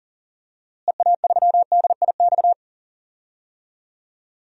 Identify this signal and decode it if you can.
Morse code, heard as a keyed tone, about 39 wpm, 700 Hz. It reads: EA3DIX
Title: DXCC-100 Call Signs - 6 Characters - Repeated 3x - 40wpm